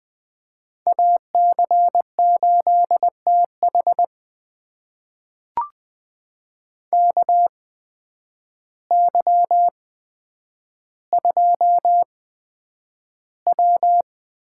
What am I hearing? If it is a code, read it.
Morse code, heard as a keyed tone, about 20 wpm, spaced as for 9 wpm, 700 Hz. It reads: AC8TH KY2W
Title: USA Call Signs - 8x Character Spacing - 20wpm